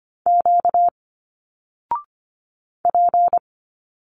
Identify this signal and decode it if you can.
Morse code, heard as a keyed tone, about 25 wpm, 700 Hz. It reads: Q P
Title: Single Letters 25wpm